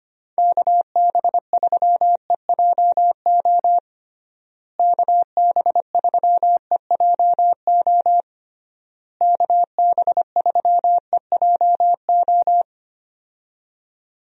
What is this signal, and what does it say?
Morse code, heard as a keyed tone, about 25 wpm, 700 Hz. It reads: KB3EJO KB3EJO KB3EJO
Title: USA Call Signs - 6 Characters - Repeated 3x - 25wpm